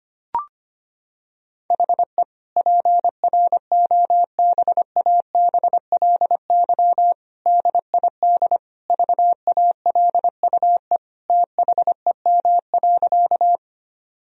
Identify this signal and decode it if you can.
Morse code, heard as a keyed tone, about 25 wpm, 700 Hz. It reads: HE PROBABLY DID VALUE THEM.
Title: Sentences from Top 1000 Words - Review Part 4 - 25wpm